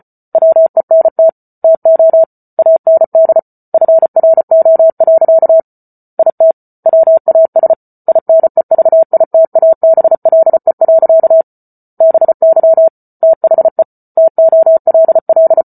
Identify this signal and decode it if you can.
Morse code, heard as a keyed tone, about 34 wpm, 650 Hz. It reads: WENT TO AND FRO. IT WAS INEVITABLE. BY THE TOLL
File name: Book Two / 08-Chapter 8 - 34 WPM